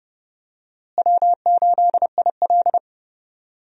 Morse code, heard as a keyed tone, about 31 wpm, 700 Hz. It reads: W8IL
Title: USA Call Signs - 4 Characters - Repeated 2x - 30wpm